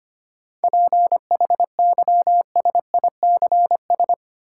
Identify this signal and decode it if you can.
Morse code, heard as a keyed tone, about 25 wpm, 700 Hz. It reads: PHYSICS
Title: Sentences from Top 1500 Words - New Word Drill Alt - 25wpm